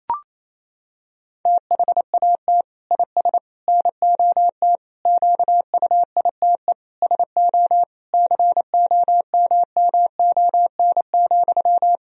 Morse code, heard as a keyed tone, about 28 wpm, 700 Hz. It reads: THAT IS NOT QUITE SO COMMON,
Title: Sentences from Top 1000 Words - Review Part 3 - 28wpm